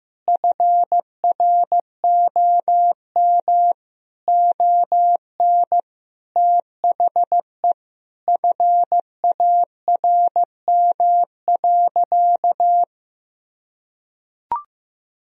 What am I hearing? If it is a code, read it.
Morse code, heard as a keyed tone, about 16 wpm, 700 Hz. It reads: FROM ON THE FARM.
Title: Sentences from Top 1000 Words - Review Part 4 - 15wpm